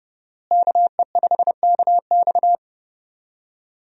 Morse code, heard as a keyed tone, about 30 wpm, 700 Hz. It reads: KE5KX
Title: USA Call Signs - 5 Characters - Repeated 3x - 30wpm